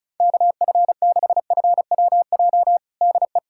Morse code, heard as a keyed tone, about 34 wpm, 700 Hz. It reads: KF6FWJ DE
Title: USA Callsign DE Callsign - 6 Characters - 35wpm